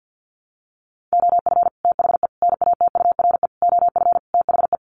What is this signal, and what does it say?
Morse code, heard as a keyed tone, about 50 wpm, 700 Hz. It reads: OF THE NATURE OF THE
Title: Sets of 5 Words Encore2 50wpm